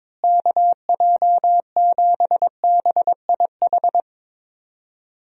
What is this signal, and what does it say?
Morse code, heard as a keyed tone, about 22 wpm, 700 Hz. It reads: KJ7BIH
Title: USA Call Signs - 6 Characters - Repeated 3x - 22wpm